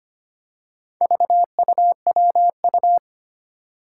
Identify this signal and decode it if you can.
Morse code, heard as a keyed tone, about 25 wpm, 700 Hz. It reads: VUWU
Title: Four Challenging Letter-Number Combinations 25wpm